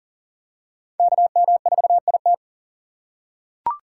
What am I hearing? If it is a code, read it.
Morse code, heard as a keyed tone, about 37 wpm, 700 Hz. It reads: KM4IT
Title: USA Call Signs - 5 Characters - 6x Character Spacing - 40wpm